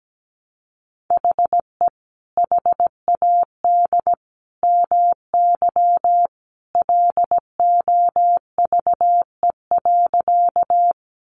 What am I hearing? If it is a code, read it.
Morse code, heard as a keyed tone, about 17 wpm, 700 Hz. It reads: HE HAD MY LOVE.
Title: Sentences from Top 400 Words 17wpm